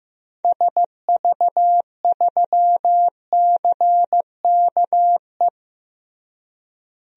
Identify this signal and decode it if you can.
Morse code, heard as a keyed tone, about 15 wpm, 700 Hz. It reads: SV3CKE
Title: DXCC-100 Call Signs - 6 Characters - Repeated 3x - 15wpm